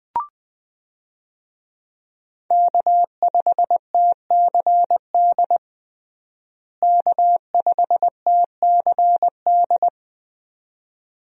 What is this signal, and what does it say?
Morse code, heard as a keyed tone, about 20 wpm, 700 Hz. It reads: K5TCD K5TCD
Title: USA Call Signs - 5 Characters - Repeated 2x - 20wpm